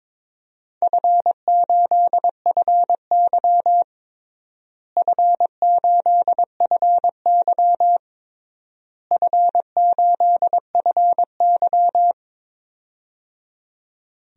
Morse code, heard as a keyed tone, about 22 wpm, 700 Hz. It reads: F8FY F8FY F8FY
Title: DXCC-100 Call Signs - 4 Characters - Repeated 3x - 22wpm